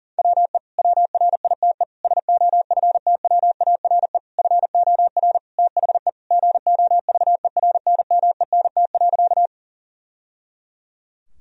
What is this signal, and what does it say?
Morse code, heard as a keyed tone, about 40 wpm, 700 Hz. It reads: WE WRITE SOFTWARE FOR THE GOVERNMENT.
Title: Sentences from Top 1000 Words - Review Part 3 - 40wpm